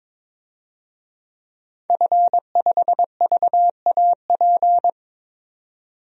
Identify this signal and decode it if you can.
Morse code, heard as a keyed tone, about 22 wpm, 700 Hz. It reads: F5VAP
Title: DXCC-100 Call Signs - 5 Characters - 22wpm